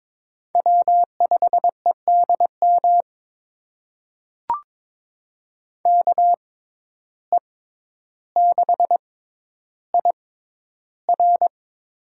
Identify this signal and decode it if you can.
Morse code, heard as a keyed tone, about 22 wpm, spaced as for 11 wpm, 700 Hz. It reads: W5EDM KE6IR
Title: USA Call Signs - 5 Characters - 6x Character Spacing - 22wpm